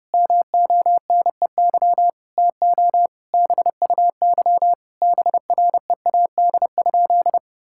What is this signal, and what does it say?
Morse code, heard as a keyed tone, about 30 wpm, 700 Hz. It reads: MONEY TO BUY BREAD?
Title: Sentences from Top 1600 Words 30wpm